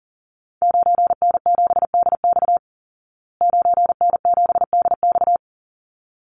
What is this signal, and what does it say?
Morse code, heard as a keyed tone, about 40 wpm, 700 Hz. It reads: 9N7DX 9N7DX
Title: Call Signs Repeated 3x - 40wpm